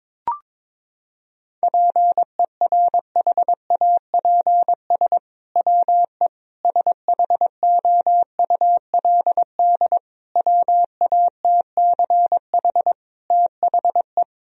Morse code, heard as a keyed tone, about 22 wpm, 700 Hz. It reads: PERHAPS WE SHOULD WATCH THE
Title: Sentences from Top 1000 Words - Review Part 4 - 22wpm